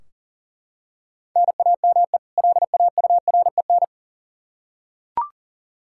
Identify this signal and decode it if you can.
Morse code, heard as a keyed tone, about 37 wpm, 700 Hz. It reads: NAME LAUREN
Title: K1USN SST QSO Practice 40wpm